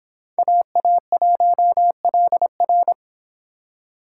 Morse code, heard as a keyed tone, about 25 wpm, 700 Hz. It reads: AA1LR